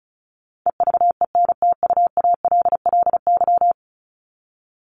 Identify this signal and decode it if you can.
Morse code, heard as a keyed tone, about 34 wpm, 700 Hz. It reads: EVENTUALLY
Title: LY Suffix - Top 100 Words with Base Words 35wpm